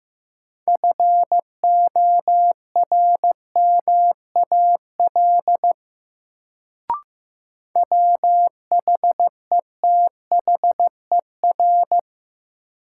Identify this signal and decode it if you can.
Morse code, heard as a keyed tone, about 15 wpm, 700 Hz. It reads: FORMAL WHETHER
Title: Sentences from Top 1300 Words - New Word Drill Alt - 15wpm